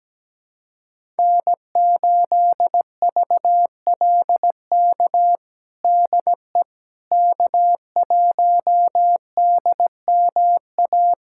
Morse code, heard as a keyed tone, about 17 wpm, 700 Hz. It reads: N8VLK DE K1DMA
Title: USA Callsign DE Callsign - 5 Characters - 17wpm